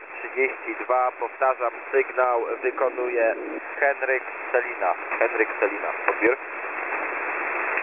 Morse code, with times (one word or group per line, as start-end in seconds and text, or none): none